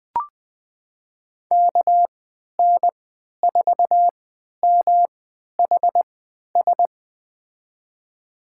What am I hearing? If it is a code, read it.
Morse code, heard as a keyed tone, about 20 wpm, spaced as for 11 wpm, 700 Hz. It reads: KN4MHS
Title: USA Call Signs - 6 Characters - 3x Character Spacing - 20wpm